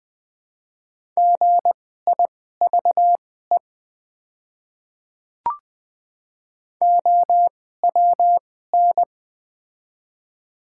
Morse code, heard as a keyed tone, about 20 wpm, spaced as for 15 wpm, 700 Hz. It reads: GIVE OWN
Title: 2x Character Spacing - Top 300 Words - 20wpm